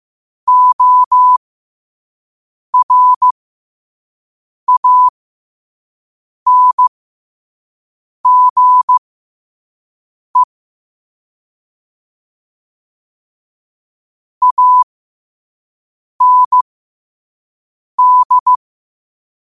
Morse code, heard as a keyed tone, about 15 wpm, spaced as for 5 wpm, 1000 Hz. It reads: ORANGE AND
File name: qotdfiles / 5wpm